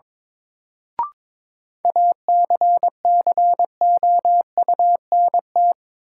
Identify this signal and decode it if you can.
Morse code, heard as a keyed tone, about 22 wpm, 700 Hz. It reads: ACCOUNT
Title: Sentences from Top 900 Words - New Word Drill - 22wpm